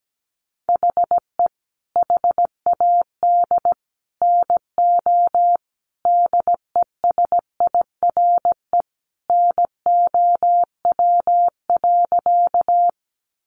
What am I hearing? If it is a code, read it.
Morse code, heard as a keyed tone, about 17 wpm, 700 Hz. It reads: HE HAD NO DESIRE NOW.
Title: Sentences from Top 1100 Words 17wpm